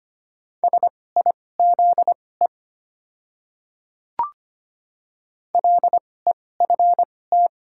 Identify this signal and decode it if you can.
Morse code, heard as a keyed tone, about 25 wpm, spaced as for 18 wpm, 700 Hz. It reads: SIZE LEFT
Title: 2x Character Spacing - Top 500 Words - 25wpm